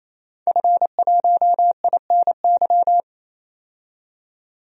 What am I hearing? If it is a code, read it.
Morse code, heard as a keyed tone, about 28 wpm, 700 Hz. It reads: F1INY
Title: DXCC-100 Call Signs - 5 Characters - 28wpm